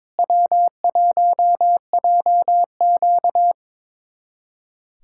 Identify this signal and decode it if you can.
Morse code, heard as a keyed tone, about 21 wpm, 700 Hz. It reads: W1JQ
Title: USA Callsign DE Callsign - 4 Characters - 22wpm